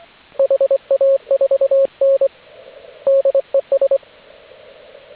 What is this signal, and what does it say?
Morse code, heard as a keyed tone, about 25 wpm, 550 Hz, with radio noise. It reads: HA4N DES